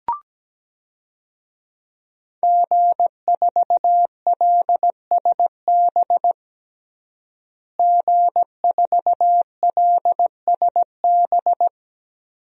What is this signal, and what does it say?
Morse code, heard as a keyed tone, about 17 wpm, 700 Hz. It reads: G4LSB G4LSB
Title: DXCC-100 Call Signs - 5 Characters - Repeated 3x - 17wpm